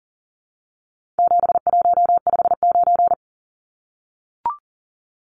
Morse code, heard as a keyed tone, about 40 wpm, 700 Hz. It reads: 7159
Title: Numbers - 4-Digit 40wpm